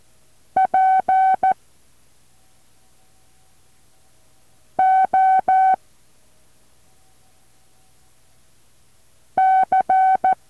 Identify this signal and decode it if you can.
Morse code, heard as a keyed tone, about 14 wpm, 760 Hz. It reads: P O C
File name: intl morse code2